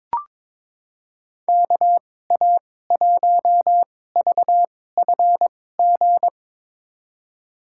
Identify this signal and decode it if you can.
Morse code, heard as a keyed tone, about 22 wpm, spaced as for 16 wpm, 700 Hz. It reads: KA1VFG
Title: USA Call Signs - 6 Characters - 2x Character Spacing - 22wpm